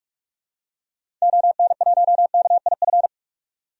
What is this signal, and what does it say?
Morse code, heard as a keyed tone, about 44 wpm, 690 Hz. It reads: ON1KIF
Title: DXCC-100 Call Signs - 6 Characters - 45wpm